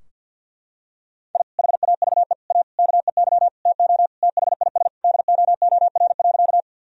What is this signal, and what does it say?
Morse code, heard as a keyed tone, about 49 wpm, 700 Hz. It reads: I HAVE A KEY TO THIS DOOR.
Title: Sentences from Top 1000 Words - Review Part 4 - 50wpm